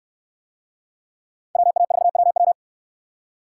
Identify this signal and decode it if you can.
Morse code, heard as a keyed tone, about 68 wpm, 690 Hz. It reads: WA4WW